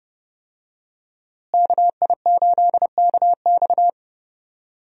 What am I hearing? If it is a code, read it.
Morse code, heard as a keyed tone, about 30 wpm, 700 Hz. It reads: KI8KX
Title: USA Call Signs - 5 Characters - Repeated 3x - 30wpm